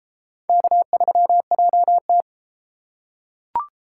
Morse code, heard as a keyed tone, about 32 wpm, 700 Hz. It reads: K3JT